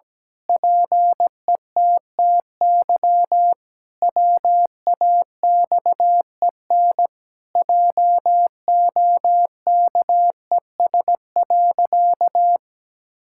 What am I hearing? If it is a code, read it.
Morse code, heard as a keyed tone, about 17 wpm, 700 Hz. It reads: PETTY WAXEN JOKES.
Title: Pangram Challenge v2 - 17wpm